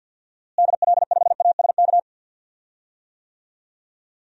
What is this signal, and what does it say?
Morse code, heard as a keyed tone, about 52 wpm, 700 Hz. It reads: DL5ASK
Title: DXCC-100 Call Signs - 6 Characters - Repeated 3x - 50wpm